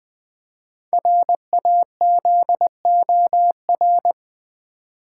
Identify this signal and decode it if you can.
Morse code, heard as a keyed tone, about 20 wpm, 700 Hz. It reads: RAZOR
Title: Easy Words with Hints - 20wpm